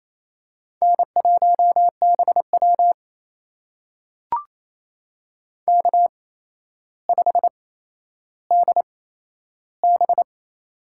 Morse code, heard as a keyed tone, about 28 wpm, spaced as for 12 wpm, 700 Hz. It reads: N1BW K5DB
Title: USA Call Signs - 4 Characters - 8x Character Spacing - 28wpm